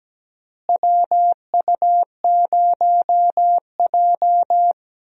Morse code, heard as a keyed tone, about 17 wpm, 700 Hz. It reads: WU0J
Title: USA Call Signs - 4 Characters - Repeated 3x - 17wpm